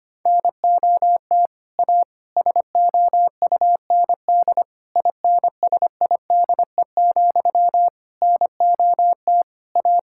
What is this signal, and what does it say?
Morse code, heard as a keyed tone, about 25 wpm, 700 Hz. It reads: NOT A SOUND INSIDE, NOT A
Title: Sentences from Top 1000 Words - Review Part 6 - 25wpm